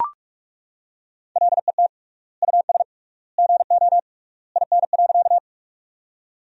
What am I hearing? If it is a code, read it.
Morse code, heard as a keyed tone, about 41 wpm, 700 Hz. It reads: LET US GO IN.
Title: 3x Word-Spacing - Sentences from Top 300 Words - 45wpm